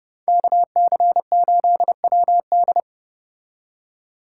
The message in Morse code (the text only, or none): KC8WD